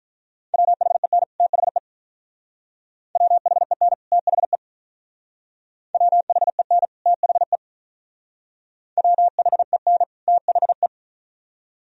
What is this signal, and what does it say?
Morse code, heard as a keyed tone, about 43 wpm, 700 Hz. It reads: WHEN THE WHEN THE WHEN THE WHEN THE
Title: Speed-Racing - Sets of 2 Words 35wpm